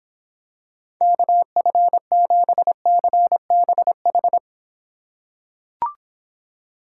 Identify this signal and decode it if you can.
Morse code, heard as a keyed tone, about 26 wpm, 700 Hz. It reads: KF7CBH